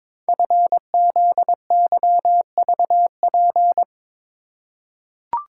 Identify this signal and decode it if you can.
Morse code, heard as a keyed tone, about 22 wpm, 700 Hz. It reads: FZYVP
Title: Code Groups - 5 Letter:Numbers - 22wpm